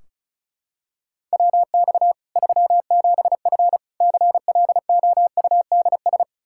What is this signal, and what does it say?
Morse code, heard as a keyed tone, about 34 wpm, 700 Hz. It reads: WX 37F CLOUDS
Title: Practice QSO - Exchange 2 - Info - 35wpm